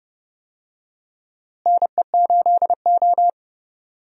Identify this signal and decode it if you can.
Morse code, heard as a keyed tone, about 31 wpm, 700 Hz. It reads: NE8O
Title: USA Call Signs - 4 Characters - Repeated 2x - 30wpm